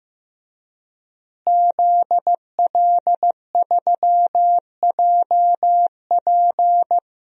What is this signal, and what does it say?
Morse code, heard as a keyed tone, about 16 wpm, 700 Hz. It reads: ZL3JP
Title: DXCC-100 Call Signs - 5 Characters - Repeated 2x - 15wpm